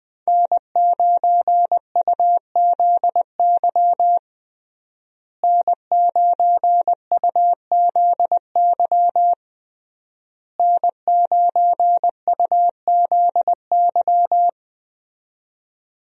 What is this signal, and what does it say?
Morse code, heard as a keyed tone, about 20 wpm, 700 Hz. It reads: N9UZY N9UZY N9UZY
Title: USA Call Signs - 5 Characters - Repeated 3x - 20wpm